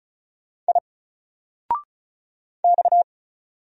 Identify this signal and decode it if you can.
Morse code, heard as a keyed tone, about 33 wpm, 700 Hz. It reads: I X
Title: Single Letter-Number 35wpm